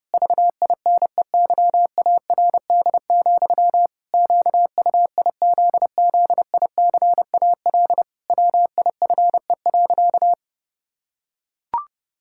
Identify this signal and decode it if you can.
Morse code, heard as a keyed tone, about 30 wpm, 700 Hz. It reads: VINEYARD, QUIZZICAL WIFE.
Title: Pangram Challenge v2 - 30wpm